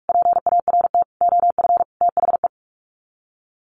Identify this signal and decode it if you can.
Morse code, heard as a keyed tone, about 44 wpm, 700 Hz. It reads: PART OF THE
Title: Sets of 5 Words Encore 45wpm